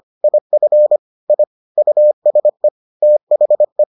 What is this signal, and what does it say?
Morse code, heard as a keyed tone, about 25 wpm, 600 Hz. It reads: IF I USE THE